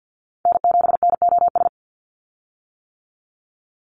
Morse code, heard as a keyed tone, about 49 wpm, 700 Hz. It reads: N7NOS
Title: USA Call Signs 50wpm